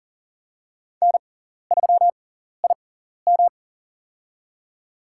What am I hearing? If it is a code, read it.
Morse code, heard as a keyed tone, about 41 wpm, spaced as for 14 wpm, 700 Hz. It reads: N3IM
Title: USA Call Signs - 4 Characters - 6x Character Spacing - 40wpm